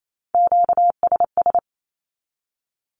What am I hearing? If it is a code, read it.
Morse code, heard as a keyed tone, about 28 wpm, 700 Hz. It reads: QSS
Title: Q-Codes 28wpm